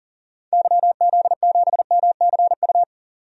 Morse code, heard as a keyed tone, about 40 wpm, 700 Hz. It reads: YZ7MCU
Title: DXCC-100 Call Signs - 6 Characters - 40wpm